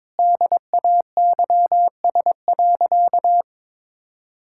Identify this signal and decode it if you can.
Morse code, heard as a keyed tone, about 22 wpm, 700 Hz. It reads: DAYS.
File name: practice / 22wpm / Sentences with 3-Word Phrases - 22wpm